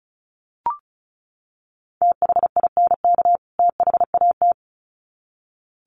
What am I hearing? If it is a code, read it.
Morse code, heard as a keyed tone, about 34 wpm, 700 Hz. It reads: THINK THAT
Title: Sets of 2 Words 35wpm